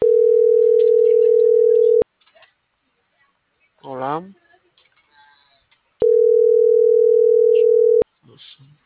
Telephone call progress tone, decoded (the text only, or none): ringback tone